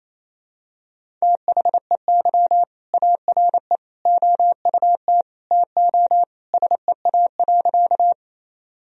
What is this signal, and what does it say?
Morse code, heard as a keyed tone, about 28 wpm, 700 Hz. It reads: THEY ARE OUT TO SEA.
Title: Sentences from Top 1000 Words - Review Part 1 - 28wpm